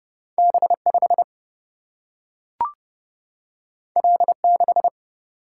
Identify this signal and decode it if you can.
Morse code, heard as a keyed tone, about 30 wpm, 700 Hz. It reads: B 5 L 6
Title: Single-Letter Single-Number 30wpm